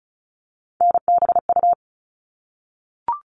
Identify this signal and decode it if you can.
Morse code, heard as a keyed tone, about 34 wpm, 700 Hz. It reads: NBU
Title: All Cut Numbers - 3-Digit 35wpm